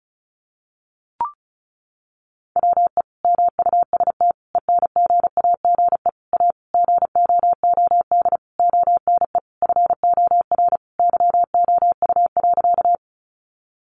Morse code, heard as a keyed tone, about 34 wpm, 700 Hz. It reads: WE MUST ENGAGE A GOOD ONE FOR YOU.
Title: Sentences from Top 1500 Words 35wpm